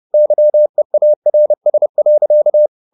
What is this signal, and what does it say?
Morse code, heard as a keyed tone, about 30 wpm, 600 Hz. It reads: YEARS.
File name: the old man and the sea / the old man and the sea 1 30wpm